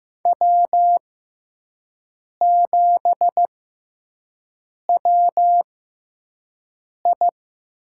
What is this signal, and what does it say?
Morse code, heard as a keyed tone, about 15 wpm, spaced as for 5 wpm, 700 Hz. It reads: W7WI
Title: USA Call Signs - 5 Characters - 6x Character Spacing - 15wpm